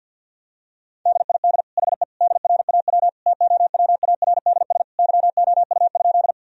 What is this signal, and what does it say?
Morse code, heard as a keyed tone, about 52 wpm, 700 Hz. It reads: DID HE DRAW TOWARDS YOU?
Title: Sentences from Top 1000 Words - Review Part 2 - 50wpm